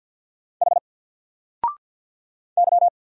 Morse code, heard as a keyed tone, about 50 wpm, 700 Hz. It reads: H Y
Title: Single Letter-Number 50wpm